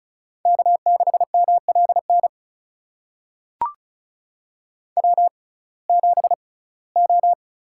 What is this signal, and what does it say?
Morse code, heard as a keyed tone, about 36 wpm, spaced as for 21 wpm, 700 Hz. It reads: K6MLN W7O
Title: USA Call Signs - 5 Characters - 6x Character Spacing - 35wpm